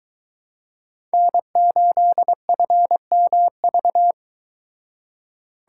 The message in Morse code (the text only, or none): N8FMV